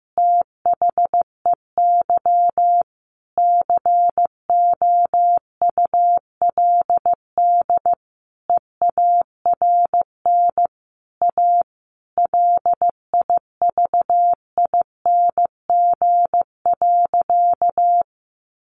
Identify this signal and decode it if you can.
Morse code, heard as a keyed tone, about 15 wpm, 700 Hz. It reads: THEY COULD EARN A LIVING.
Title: Sentences from Top 1500 Words 15wpm